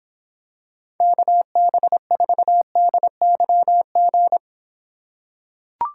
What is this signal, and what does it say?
Morse code, heard as a keyed tone, about 26 wpm, 700 Hz. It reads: KB4DYG